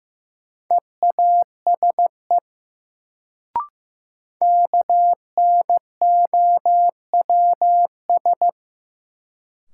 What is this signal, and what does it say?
Morse code, heard as a keyed tone, about 15 wpm, 700 Hz. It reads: EASE KNOWS
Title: Sentences from Top 1000 Words - Review Part 1 - New Word Drill - 15wpm